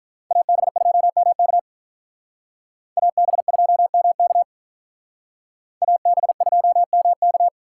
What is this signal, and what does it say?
Morse code, heard as a keyed tone, about 46 wpm, 700 Hz. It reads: AB2MK AB2MK AB2MK